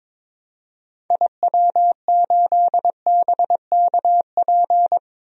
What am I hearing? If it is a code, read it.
Morse code, heard as a keyed tone, about 22 wpm, 700 Hz. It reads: IW8BKP
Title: DXCC-100 Call Signs - 6 Characters - 22wpm